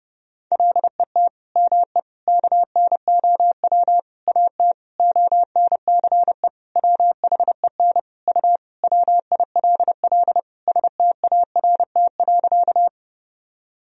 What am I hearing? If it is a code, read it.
Morse code, heard as a keyed tone, about 32 wpm, 700 Hz. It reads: LET ME KNOW AT ONCE WHEN U WILL START.
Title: Sentences from Top 300 Words and Abbreviations 30wpm